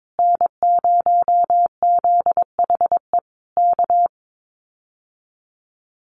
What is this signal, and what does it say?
Morse code, heard as a keyed tone, about 22 wpm, 700 Hz. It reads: N0ZHE K
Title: Common QSO Phrases Encore 22wpm